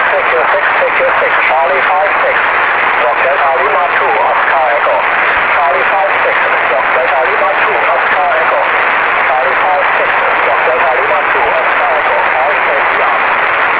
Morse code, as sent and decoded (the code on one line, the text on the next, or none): none